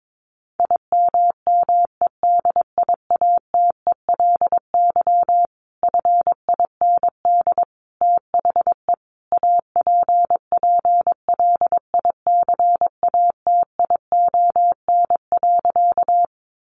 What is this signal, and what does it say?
Morse code, heard as a keyed tone, about 22 wpm, 700 Hz. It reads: IMMEDIATELY FIND THE APPLICATION.
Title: Sentences from Top 1200 Words 22wpm